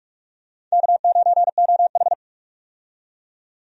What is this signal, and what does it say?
Morse code, heard as a keyed tone, about 43 wpm, 700 Hz. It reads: K9OH
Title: USA Call Signs - 4 Characters - 45wpm